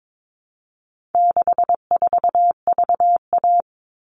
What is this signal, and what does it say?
Morse code, heard as a keyed tone, about 22 wpm, 700 Hz. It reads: 64VA
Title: All Cut Numbers - 4-Digit 22wpm